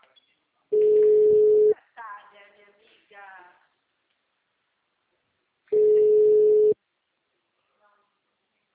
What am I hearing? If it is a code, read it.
Ringback tone (Europe, 425 Hz), with speech.